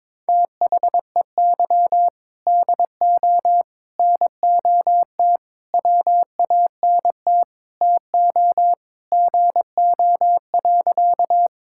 Morse code, heard as a keyed tone, about 22 wpm, 700 Hz. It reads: THEY DO NOT WANT TO GO.
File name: practice / 22wpm / Sentences with 3-Word Phrases - 22wpm